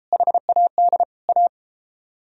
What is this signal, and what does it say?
Morse code, heard as a keyed tone, about 33 wpm, 700 Hz. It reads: HAD A